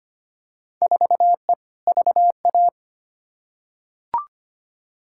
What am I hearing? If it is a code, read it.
Morse code, heard as a keyed tone, about 25 wpm, 700 Hz. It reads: 4E VA
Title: Field Day Exchanges Encore - 25wpm